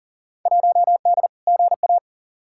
Morse code, heard as a keyed tone, about 37 wpm, 700 Hz. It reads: 1D GA
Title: Field Day Exchanges - 40wpm